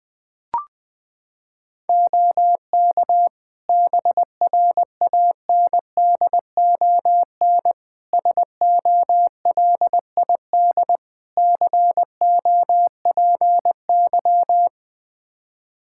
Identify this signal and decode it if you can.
Morse code, heard as a keyed tone, about 20 wpm, 700 Hz. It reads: OK BRANDON SOLID COPY
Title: K1USN SST QSO Practice 20wpm